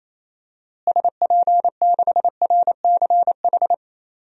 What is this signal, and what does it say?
Morse code, heard as a keyed tone, about 28 wpm, 700 Hz. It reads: SP6RCH
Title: DXCC-100 Call Signs - 6 Characters - Repeated 3x - 28wpm